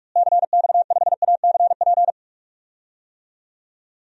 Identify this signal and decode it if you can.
Morse code, heard as a keyed tone, about 46 wpm, 700 Hz. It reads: CX5ACP
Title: DXCC-100 Call Signs - 6 Characters - Repeated 2x - 45wpm